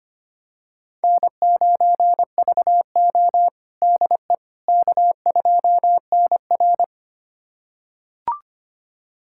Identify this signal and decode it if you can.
Morse code, heard as a keyed tone, about 25 wpm, 700 Hz. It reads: N9VO DE K2NR
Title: USA Callsign DE Callsign - 4 Characters - 25wpm